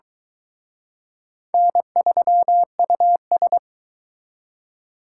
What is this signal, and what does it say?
Morse code, heard as a keyed tone, about 23 wpm, 700 Hz. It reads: N3US